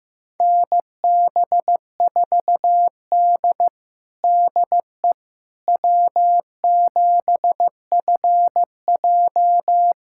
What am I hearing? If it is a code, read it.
Morse code, heard as a keyed tone, about 15 wpm, 700 Hz. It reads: NB4D DE W7FJ
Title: USA Callsign DE Callsign - 4 Characters - 15wpm